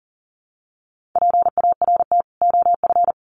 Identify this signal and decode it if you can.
Morse code, heard as a keyed tone, about 37 wpm, 700 Hz. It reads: PART OF
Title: Sets of 2 Words 40wpm